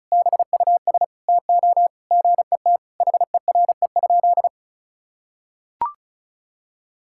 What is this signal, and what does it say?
Morse code, heard as a keyed tone, about 36 wpm, 700 Hz. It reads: BUS TO GET HERE?
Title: Sentences from Top 1000 Words - Review Part 6 - 35wpm